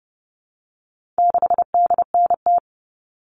Morse code, heard as a keyed tone, about 30 wpm, 700 Hz. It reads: 6DNT
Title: All Cut Numbers - 4-Digit 30wpm